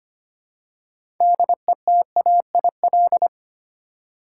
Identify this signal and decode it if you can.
Morse code, heard as a keyed tone, about 25 wpm, 700 Hz. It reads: DETAIL
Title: Six Letter Words - 25wpm